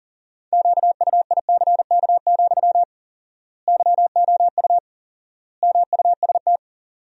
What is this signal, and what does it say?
Morse code, heard as a keyed tone, about 40 wpm, 700 Hz. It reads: QUICK, YOU MUST
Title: 4x Word-Spacing - Sentences from Top 500 Words - 40wpm